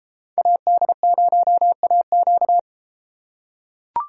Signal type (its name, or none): Morse code